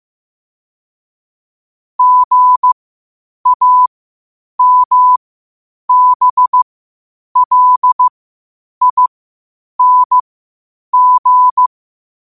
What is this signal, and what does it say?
Morse code, heard as a keyed tone, about 15 wpm, spaced as for 8 wpm, 1000 Hz. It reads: GAMBLING
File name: qotdfiles / 8wpm